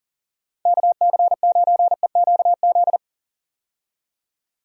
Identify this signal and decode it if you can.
Morse code, heard as a keyed tone, about 40 wpm, 700 Hz. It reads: KC9EQZ
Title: USA Call Signs - 6 Characters - 40wpm